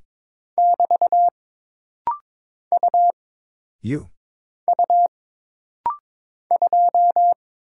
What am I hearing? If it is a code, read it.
Morse code, heard as a keyed tone, about 22 wpm, 700 Hz. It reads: <BT> U U 2
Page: This rapid-fire timing is geared to push your ICR (Instant Character Recognition) skills.